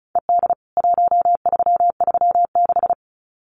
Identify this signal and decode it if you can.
Morse code, heard as a keyed tone, about 34 wpm, 700 Hz. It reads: ED 1336
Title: CWT Exchange 35wpm